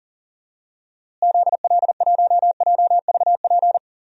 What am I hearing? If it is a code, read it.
Morse code, heard as a keyed tone, about 40 wpm, 700 Hz. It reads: ZL1JVP
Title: DXCC-100 Call Signs - 6 Characters - Repeated 3x - 40wpm